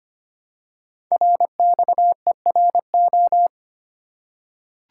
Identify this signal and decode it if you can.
Morse code, heard as a keyed tone, about 25 wpm, 700 Hz. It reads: RXERO
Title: Code Groups - 5 Letters - 25wpm